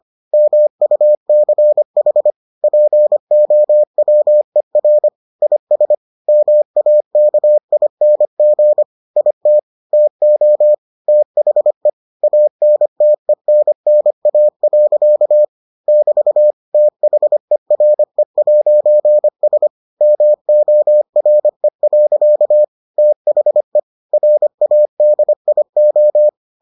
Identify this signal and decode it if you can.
Morse code, heard as a keyed tone, about 25 wpm, 600 Hz. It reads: MUCH POWER IS MAKING IT TO THE ANTENNA. <BT> THERE'S MORE. THE RADIO